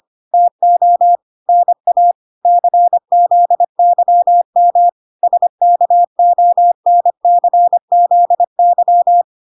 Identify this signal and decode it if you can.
Morse code, heard as a keyed tone, about 25 wpm, 700 Hz. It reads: TO NA CZYM SKONCZY